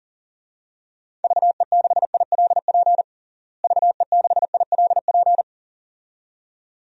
Morse code, heard as a keyed tone, about 40 wpm, 700 Hz. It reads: VE6ILP VE6ILP
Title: DXCC-100 Call Signs - 6 Characters - Repeated 2x - 40wpm